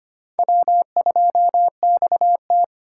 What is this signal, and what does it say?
Morse code, heard as a keyed tone, about 25 wpm, 700 Hz. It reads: W2XT
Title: USA Call Signs - 4 Characters - 8x Character Spacing - 25wpm